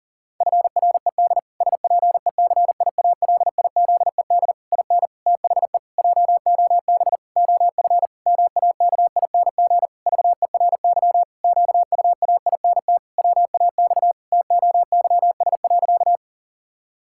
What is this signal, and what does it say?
Morse code, heard as a keyed tone, about 40 wpm, 700 Hz. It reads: FRED SPECIALIZED IN THE JOB OF MAKING VERY QUAINT WAX TOYS.
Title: Pangram Challenge v2 - 40wpm